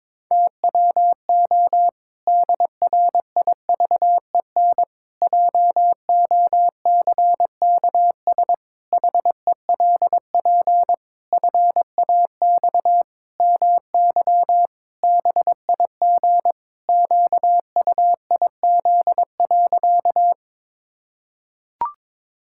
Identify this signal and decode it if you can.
Morse code, heard as a keyed tone, about 22 wpm, 700 Hz. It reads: TWO DRIVEN JOCKS HELP FAX MY BIG QUIZ.
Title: Pangram Challenge v2 - 22wpm